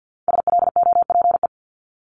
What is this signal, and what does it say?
Morse code, heard as a keyed tone, about 50 wpm, 700 Hz. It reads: SLOPE
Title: Multi-letter Phonemes Repeated 3x 50wpm